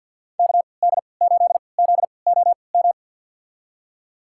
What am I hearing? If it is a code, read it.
Morse code, heard as a keyed tone, about 52 wpm, spaced as for 28 wpm, 700 Hz. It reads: KD8ZOM
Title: USA Call Signs - 3x Character Spacing - 50wpm